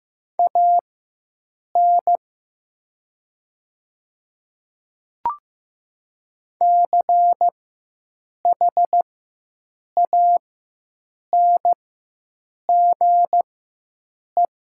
Morse code, heard as a keyed tone, about 15 wpm, spaced as for 7 wpm, 700 Hz. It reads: AN CHANGE
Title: 4x Character Spacing - Top 200 Words - 15wpm